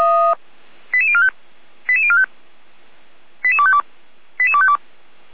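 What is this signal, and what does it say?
Five-tone selective call (ZVEI), heard as quick runs of five tones, tones of about 80 ms. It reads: 88045 88045 88252 88252
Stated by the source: ZVEI five-tone call received by SDR.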